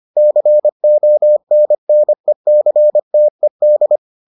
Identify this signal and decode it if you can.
Morse code, heard as a keyed tone, about 25 wpm, 600 Hz. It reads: CONNECTED